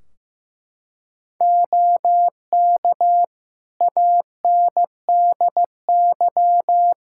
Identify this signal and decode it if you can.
Morse code, heard as a keyed tone, about 15 wpm, 700 Hz. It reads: OK ANDY
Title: K1USN SST QSO Practice 15wpm